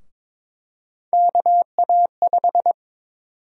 Morse code, heard as a keyed tone, about 22 wpm, 700 Hz. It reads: KA5
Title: USA Call Sign Prefixes - 22wpm